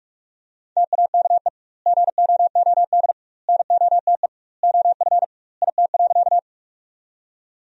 Morse code, heard as a keyed tone, about 46 wpm, 700 Hz. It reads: TAKE GOOD NOTE OF IT.
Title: 2x Word-Spacing - Sentences from Top 500 Words - 45wpm